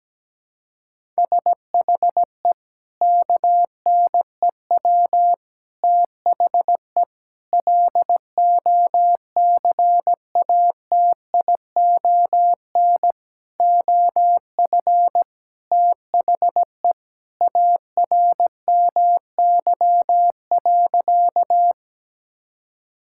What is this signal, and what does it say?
Morse code, heard as a keyed tone, about 17 wpm, 700 Hz. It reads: SHE KNEW THE LOCATION OF THE ARMY.
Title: Sentences from Top 1000 Words - Review Part 6 - 17wpm